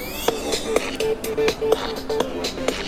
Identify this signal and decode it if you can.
There is no code.